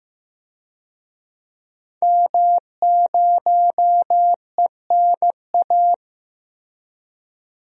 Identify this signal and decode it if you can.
Morse code, heard as a keyed tone, about 15 wpm, 700 Hz. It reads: M0ENA
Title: DXCC-100 Call Signs - 5 Characters - Repeated 3x - 15wpm